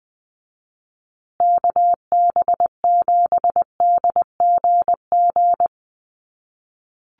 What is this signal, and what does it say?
Morse code, heard as a keyed tone, about 20 wpm, 700 Hz. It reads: KB7DGG
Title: USA Call Signs Repeated 2x 20wpm